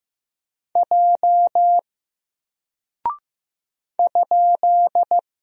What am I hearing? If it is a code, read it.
Morse code, heard as a keyed tone, about 15 wpm, 700 Hz. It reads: J ?
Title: Single Character (Letter:Number:Punctuation:Prosign) 15wpm